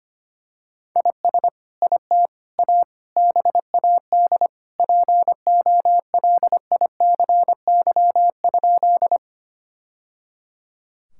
Morse code, heard as a keyed tone, about 25 wpm, 700 Hz. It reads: IS IT A BAD POLICY?
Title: Sentences from Top 1000 Words - Review Part 5 - 25wpm